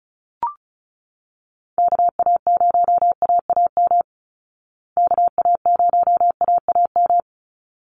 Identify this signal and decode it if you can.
Morse code, heard as a keyed tone, about 34 wpm, 700 Hz. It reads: KA0AAM KA0AAM
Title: Call Signs Repeated 3x - 35wpm